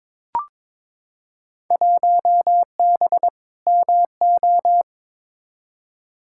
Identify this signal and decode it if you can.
Morse code, heard as a keyed tone, about 22 wpm, 700 Hz. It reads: 1B MO
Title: Field Day Exchanges - 22wpm